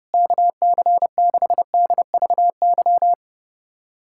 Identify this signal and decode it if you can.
Morse code, heard as a keyed tone, about 30 wpm, 700 Hz. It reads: KC6DVY
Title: USA Callsign DE Callsign 30wpm